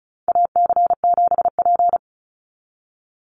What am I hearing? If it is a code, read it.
Morse code, heard as a keyed tone, about 34 wpm, 700 Hz. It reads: AC7P
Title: NA SOTA Chaser Callsigns 35wpm